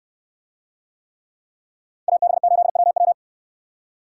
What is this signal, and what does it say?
Morse code, heard as a keyed tone, about 69 wpm, 700 Hz. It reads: AB8WW